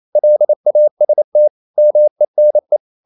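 Morse code, heard as a keyed tone, about 28 wpm, 600 Hz. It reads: LAST MENE
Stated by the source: Danish short words sent in Morse at 28 wpm, from CW med Gnister.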